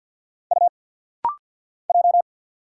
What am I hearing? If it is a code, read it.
Morse code, heard as a keyed tone, about 50 wpm, 700 Hz. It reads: U J
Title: Single Letters - Rapid-Fire 50wpm